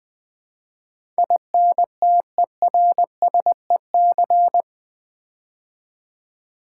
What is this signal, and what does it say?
Morse code, heard as a keyed tone, about 20 wpm, 700 Hz. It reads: INTERSEC
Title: Build A Word - 4 Letters At A Time - 12-Letter Words - 20wpm